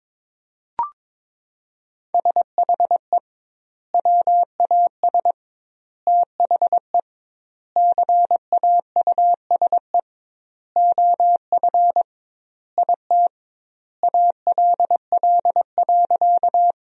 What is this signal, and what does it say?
Morse code, heard as a keyed tone, about 22 wpm, 700 Hz. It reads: SHE WAS THE CAUSE OF IT ALL.
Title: 2x Word-Spacing - Sentences from Top 200 Words - 22wpm